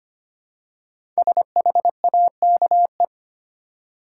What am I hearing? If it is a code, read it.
Morse code, heard as a keyed tone, about 25 wpm, 700 Hz. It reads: SHAKE
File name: 5 Letter Words - First 3 Letters Given - 25wpm